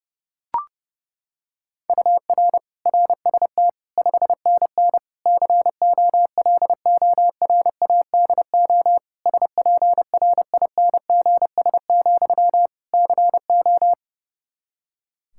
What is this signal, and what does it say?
Morse code, heard as a keyed tone, about 30 wpm, 700 Hz. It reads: UR RST 5NN COLORADO SPRINGS, CO
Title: Practice QSO - Exchange 1 - Beginning - 30wpm